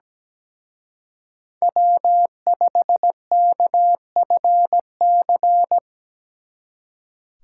Morse code, heard as a keyed tone, about 20 wpm, 700 Hz. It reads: W5KFC
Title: Speed Racing - USA 1x3 Call Signs 17wpm